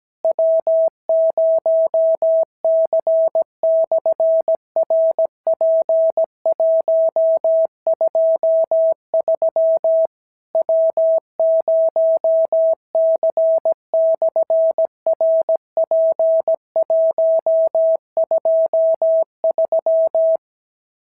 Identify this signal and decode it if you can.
Morse code, heard as a keyed tone, about 17 wpm, 690 Hz. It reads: W0C/RP123 W0C/RP123
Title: SOTA QSO Practice 17wpm